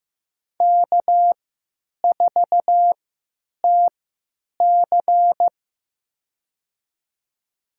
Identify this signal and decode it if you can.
Morse code, heard as a keyed tone, about 15 wpm, spaced as for 9 wpm, 700 Hz. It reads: K4TC
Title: USA Call Signs - 3x Character Spacing - 15wpm